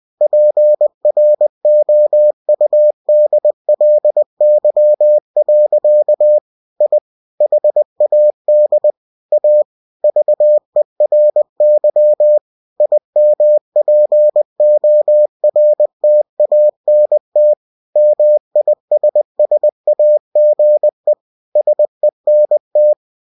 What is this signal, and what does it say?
Morse code, heard as a keyed tone, about 20 wpm, 600 Hz. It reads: PROUDLY. I HAD A VERY IMPORTANT MISSAGE SENT
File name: cw-books / winnie-the-pooh / winnie-20wpm-page-0098